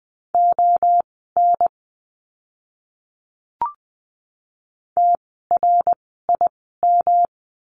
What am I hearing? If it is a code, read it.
Morse code, heard as a keyed tone, about 20 wpm, spaced as for 14 wpm, 700 Hz. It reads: ON TRIM
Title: 108 Lesson 18f - Letter m - Words 20wpm